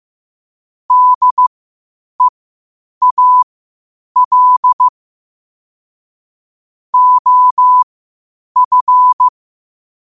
Morse code, heard as a keyed tone, about 15 wpm, spaced as for 8 wpm, 1000 Hz. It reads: DEAL OF
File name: qotdfiles / 8wpm